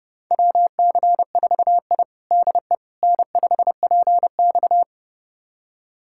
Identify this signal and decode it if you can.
Morse code, heard as a keyed tone, about 30 wpm, 700 Hz. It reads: WC4I DE N5PX
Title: USA Callsign DE Callsign - 4 Characters - 30wpm